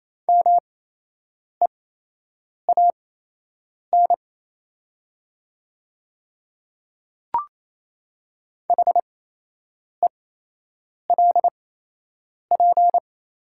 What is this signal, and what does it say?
Morse code, heard as a keyed tone, about 28 wpm, spaced as for 8 wpm, 700 Hz. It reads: MEAN HELP
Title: 8x Character Spacing - Top 200 Words - 28wpm